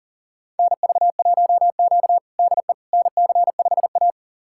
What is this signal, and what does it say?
Morse code, heard as a keyed tone, about 37 wpm, 700 Hz. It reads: NV1Q DE NC5A
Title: USA Callsign DE Callsign - 4 Characters - 40wpm